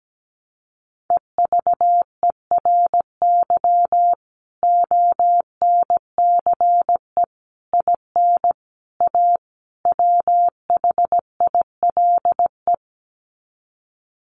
Morse code, heard as a keyed tone, about 17 wpm, 700 Hz. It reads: EVERY ONCE IN A WHILE
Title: Sets of 5 Words 17wpm